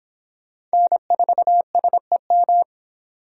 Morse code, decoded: N4SEM

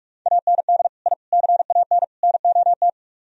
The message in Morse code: AND I CAN NOT

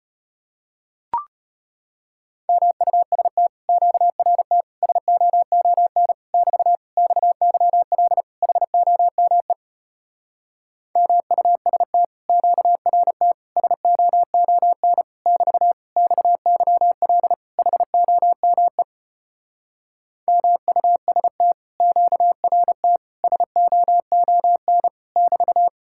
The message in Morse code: MUST QRT SOON <BT> XYL HOME MUST QRT SOON <BT> XYL HOME MUST QRT SOON <BT>

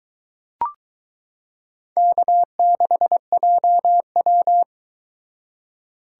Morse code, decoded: K6JW

